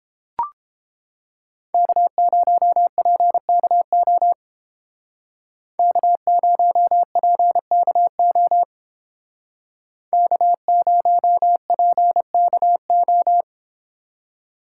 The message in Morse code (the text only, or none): K0PKO K0PKO K0PKO